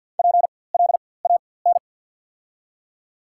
Morse code, heard as a keyed tone, about 53 wpm, spaced as for 23 wpm, 690 Hz. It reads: PLAN
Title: 4x Character Spacing - Top 500 Words - 50wpm